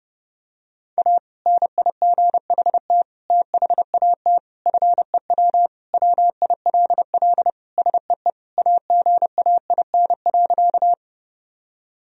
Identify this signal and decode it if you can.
Morse code, heard as a keyed tone, about 32 wpm, 700 Hz. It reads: A NIGHT THAT FEW WILL SEE AGAIN.
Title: Sentences from Top 1000 Words - Review Part 1 - 30wpm